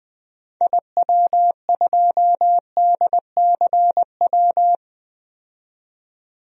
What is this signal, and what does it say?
Morse code, heard as a keyed tone, about 20 wpm, 700 Hz. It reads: IW2DCW
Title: DXCC-100 Call Signs - 6 Characters - 20wpm